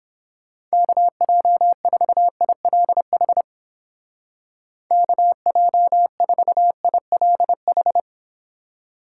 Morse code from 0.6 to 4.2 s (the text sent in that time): KJ4ILH